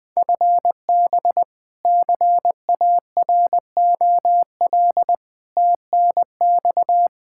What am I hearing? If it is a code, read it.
Morse code, heard as a keyed tone, about 20 wpm, 700 Hz. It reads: FB CAROL TNX
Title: Practice QSO - Exchange 3 - More Info - 20wpm